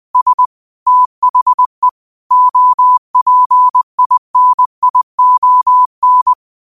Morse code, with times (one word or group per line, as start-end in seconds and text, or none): 0.1-0.5 S
0.9-1.9 THE
2.3-6.3 OPINION